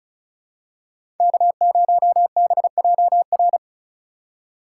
Morse code, heard as a keyed tone, about 34 wpm, 690 Hz. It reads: K0BJR
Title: USA Call Signs - 4x Character Spacing - 35wpm